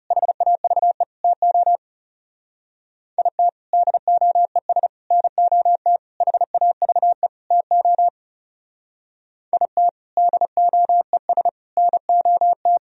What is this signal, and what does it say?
Morse code, heard as a keyed tone, about 36 wpm, 700 Hz. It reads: HAVE TO IT DOES NOT HAVE TO IT DOES NOT
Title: Speed-Racing - Sets of 5 Words 30wpm